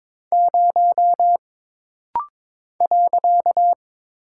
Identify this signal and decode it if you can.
Morse code, heard as a keyed tone, about 22 wpm, 700 Hz. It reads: 0 .
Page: This rapid-fire timing is geared to push your ICR (Instant Character Recognition) skills.